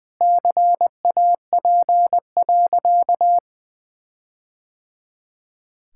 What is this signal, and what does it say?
Morse code, heard as a keyed tone, about 21 wpm, 700 Hz. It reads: CAP.
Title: Sentences from Top 1600 Words 20wpm